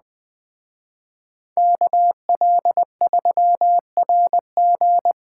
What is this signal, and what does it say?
Morse code, heard as a keyed tone, about 20 wpm, 700 Hz. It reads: KL3RG